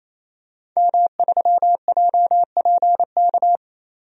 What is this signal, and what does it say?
Morse code, heard as a keyed tone, about 28 wpm, 700 Hz. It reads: M3JPK
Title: DXCC-100 Call Signs - 5 Characters - 28wpm